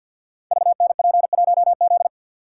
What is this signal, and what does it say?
Morse code, heard as a keyed tone, about 51 wpm, 700 Hz. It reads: VNP1Z
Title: Code Groups - 5 Letter:Numbers - 50wpm